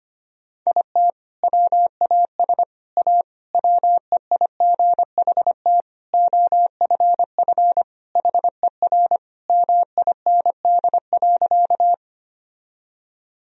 Morse code, heard as a keyed tone, about 25 wpm, 700 Hz. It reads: IT WAS A WEIGHT OFF HER MIND.
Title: Sentences from Top 1000 Words - Review Part 3 - 25wpm